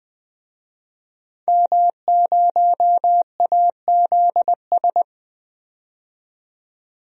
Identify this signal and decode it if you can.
Morse code, heard as a keyed tone, about 20 wpm, 700 Hz. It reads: M0AZS
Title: DXCC-100 Call Signs - 5 Characters - 20wpm